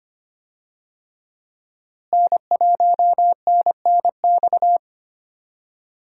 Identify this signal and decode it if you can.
Morse code, heard as a keyed tone, about 25 wpm, 700 Hz. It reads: N1NNX
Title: USA Call Signs - 5 Characters - Repeated 3x - 25wpm